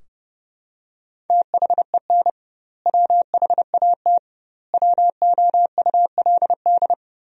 Morse code, heard as a keyed tone, about 30 wpm, 700 Hz. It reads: THEN WHAT WOULD
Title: 2x Word-Spacing - Sentences from Top 400 Words - 30wpm